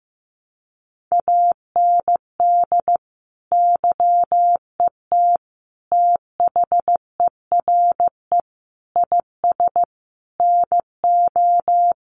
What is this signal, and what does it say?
Morse code, heard as a keyed tone, about 15 wpm, 700 Hz. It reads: AND YET THERE IS NO
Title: Sentences from Top 1000 Words 15wpm